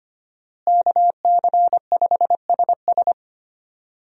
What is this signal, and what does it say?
Morse code, heard as a keyed tone, about 25 wpm, 700 Hz. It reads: KC5SS
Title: USA Call Signs - 5 Characters - Repeated 2x - 25wpm